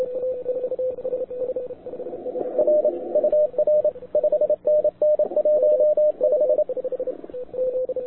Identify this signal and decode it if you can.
Morse code, heard as a keyed tone, about 27 wpm, 600 Hz, with radio noise. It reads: R UR 5NN 15
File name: wwcw08 / k5zd wwcw08 1400z